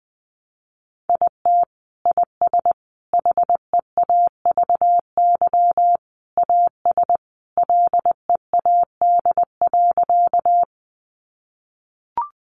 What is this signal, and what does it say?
Morse code, heard as a keyed tone, about 19 wpm, 700 Hz. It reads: IT IS HEAVY AS LEAD.
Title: Sentences from Top 1100 Words 20wpm